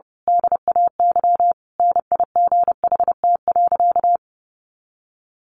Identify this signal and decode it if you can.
Morse code, heard as a keyed tone, about 28 wpm, 700 Hz. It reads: DAY NIGHT.
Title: A Sound Of Thunder 30wpm